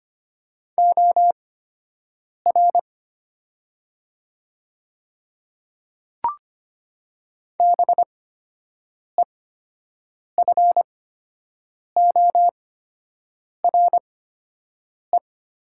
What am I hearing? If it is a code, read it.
Morse code, heard as a keyed tone, about 25 wpm, spaced as for 7 wpm, 700 Hz. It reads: OR BEFORE
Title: 8x Character Spacing - Top 200 Words - 25wpm